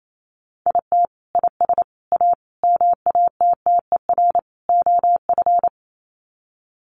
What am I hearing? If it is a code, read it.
Morse code, heard as a keyed tone, about 28 wpm, 700 Hz. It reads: IT IS A MATTER OF
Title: Sets of 5 Words Encore2 28wpm